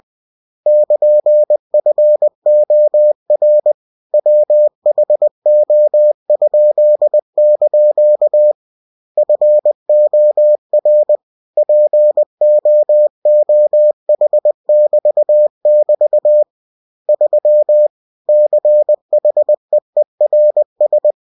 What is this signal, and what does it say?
Morse code, heard as a keyed tone, about 20 wpm, 600 Hz. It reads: <KN>FOR WHO?) FOR POOH<BT><BT> 3 CHEERS